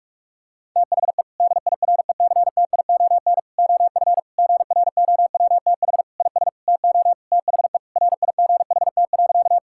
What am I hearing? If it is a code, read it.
Morse code, heard as a keyed tone, about 46 wpm, 700 Hz. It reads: THE DIRECTION OF GROWTH IS TO THE RIGHT.
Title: Sentences from Top 1000 Words - Review Part 5 - 45wpm